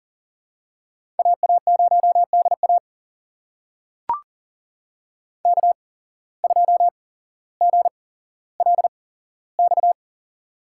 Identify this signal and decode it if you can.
Morse code, heard as a keyed tone, about 38 wpm, spaced as for 17 wpm, 700 Hz. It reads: AA0DA K2GLX
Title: USA Call Signs - 5 Characters - 8x Character Spacing - 40wpm